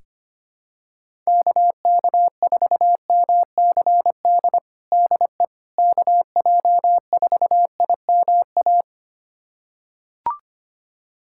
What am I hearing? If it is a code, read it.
Morse code, heard as a keyed tone, about 25 wpm, 700 Hz. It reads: KK4MCD DE KJ4IMA
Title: USA Callsign DE Callsign - 6 Characters - 25wpm